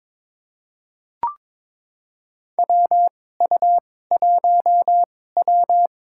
Morse code, heard as a keyed tone, about 23 wpm, spaced as for 16 wpm, 700 Hz. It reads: WU1W
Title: USA Call Signs - 4 Characters - 2x Character Spacing - 22wpm